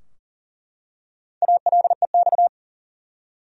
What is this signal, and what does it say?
Morse code, heard as a keyed tone, about 40 wpm, 700 Hz. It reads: ALEX
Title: USA Common Names - Max 5 Letters - 40wpm